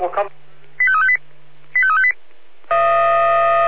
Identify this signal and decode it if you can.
Five-tone selective call (ZVEI), heard as quick runs of five tones, tones of about 70 ms, with speech. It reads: 85358 85358